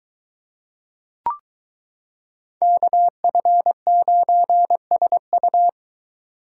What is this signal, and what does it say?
Morse code, heard as a keyed tone, about 21 wpm, 700 Hz. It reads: KF9SU